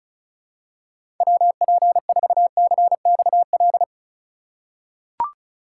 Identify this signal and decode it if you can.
Morse code, heard as a keyed tone, about 36 wpm, 690 Hz. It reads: WP4CXL
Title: USA Call Signs - 6 Characters - 2x Character Spacing - 35wpm